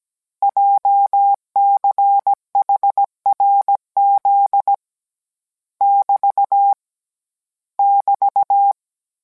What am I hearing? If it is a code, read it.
Morse code, heard as a keyed tone, about 17 wpm, 800 Hz. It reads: JCHRZ <BT> <BT>